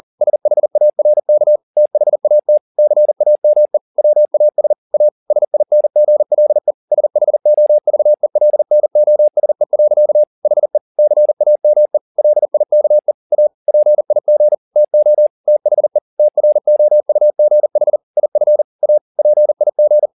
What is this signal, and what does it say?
Morse code, heard as a keyed tone, about 40 wpm, 600 Hz. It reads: SHARK THAT CAME WAS A SINGLE SHOVELNOSE. HE CAME LIKE A PIG TO THE TROUGH IF A PIG
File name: the old man and the sea 2 40wpm